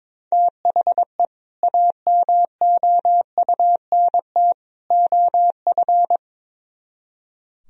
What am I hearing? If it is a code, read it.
Morse code, heard as a keyed tone, about 22 wpm, 700 Hz. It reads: THE AMOUNT OF
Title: Speed-Racing - Sets of 3 Words 22wpm